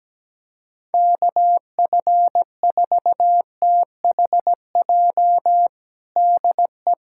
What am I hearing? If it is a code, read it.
Morse code, heard as a keyed tone, about 17 wpm, 700 Hz. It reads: KF4THJ DE
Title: USA Callsign DE Callsign 17wpm